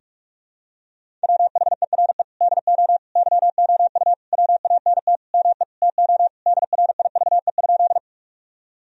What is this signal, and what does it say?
Morse code, heard as a keyed tone, about 46 wpm, 700 Hz. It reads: WHERE DO YOU WANT ME TO DRIVE?
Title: Sentences from Top 1000 Words - Review Part 2 - 45wpm